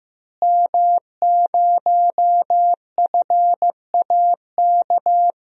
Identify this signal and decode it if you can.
Morse code, heard as a keyed tone, about 15 wpm, 700 Hz. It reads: M0FAK
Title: DXCC-100 Call Signs - 5 Characters - Repeated 3x - 15wpm